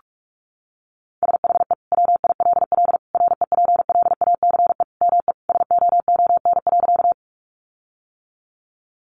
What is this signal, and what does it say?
Morse code, heard as a keyed tone, about 44 wpm, 700 Hz. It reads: SHE WILL REPLACE ME SOON.
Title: Sentences from Top 1400 Words 45wpm